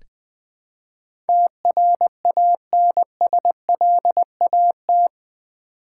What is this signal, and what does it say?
Morse code, heard as a keyed tone, about 20 wpm, 700 Hz. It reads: TRANSLAT
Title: Build A Word - 4 Letters At A Time - 11-Letter Words - 20wpm